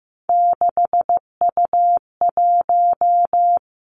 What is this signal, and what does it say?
Morse code, heard as a keyed tone, about 15 wpm, 700 Hz. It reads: 6U1
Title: Three Challenging Letter-Number Combinations 15wpm